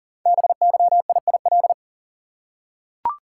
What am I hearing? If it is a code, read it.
Morse code, heard as a keyed tone, about 41 wpm, 690 Hz. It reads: BYIIL
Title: Code Groups - 5 Letters - 40wpm